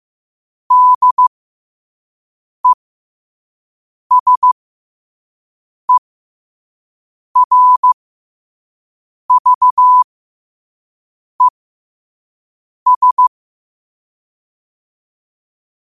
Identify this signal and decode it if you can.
Morse code, heard as a keyed tone, about 15 wpm, spaced as for 5 wpm, 1000 Hz. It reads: DESERVES
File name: qotdfiles / 5wpm